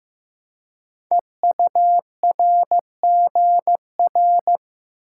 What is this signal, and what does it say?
Morse code, heard as a keyed tone, about 15 wpm, 690 Hz. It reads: EURGR
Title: Code Groups - 5 Letters - 15wpm